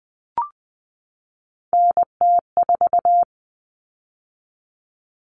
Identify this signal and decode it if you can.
Morse code, heard as a keyed tone, about 20 wpm, 700 Hz. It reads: NT4
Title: All Cut Numbers - 3-Digit 20wpm